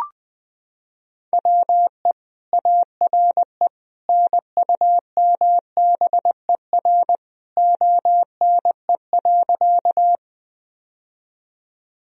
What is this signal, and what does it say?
Morse code, heard as a keyed tone, about 20 wpm, 700 Hz. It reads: WE ARE NUMBER ONE.
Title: Sentences from Top 1000 Words - Review Part 1 - 20wpm